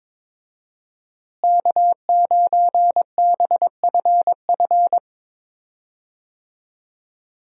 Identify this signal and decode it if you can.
Morse code, heard as a keyed tone, about 22 wpm, 700 Hz. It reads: K9BFF
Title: USA Call Signs - 5 Characters - Repeated 2x - 22wpm